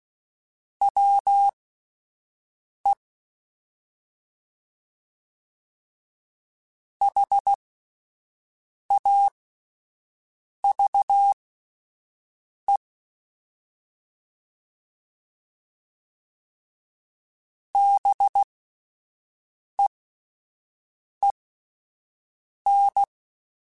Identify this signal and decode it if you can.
Morse code, heard as a keyed tone, about 16 wpm, spaced as for 5 wpm, 800 Hz. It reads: WE HAVE BEEN
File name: (spaced) / QSOs Spaced-mp3s / WD0BGP 1 @10 sp